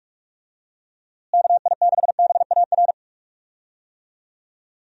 Morse code, heard as a keyed tone, about 46 wpm, 700 Hz. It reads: KI6BAR
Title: USA Call Signs - 6 Characters - Repeated 2x - 45wpm